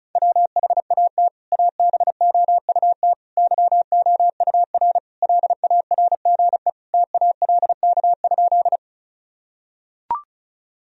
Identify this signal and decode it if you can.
Morse code, heard as a keyed tone, about 36 wpm, 700 Hz. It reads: WHAT ABOUT YOUR LARGE TALK?
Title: Sentences from Top 1000 Words - Review Part 1 - 35wpm